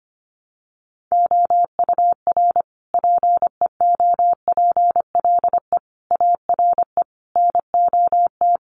Morse code, heard as a keyed tone, about 25 wpm, 700 Hz. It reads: OUR PEOPLE ARE NOT
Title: Sentences from Top 1500 Words 25wpm